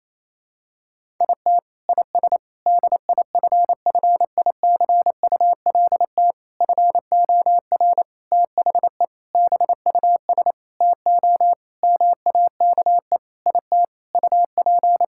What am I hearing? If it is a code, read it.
Morse code, heard as a keyed tone, about 28 wpm, 700 Hz. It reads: IT IS DIFFICULT FOR THE BUS TO MAKE IT UP
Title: Sentences from Top 1000 Words - Review Part 6 - 28wpm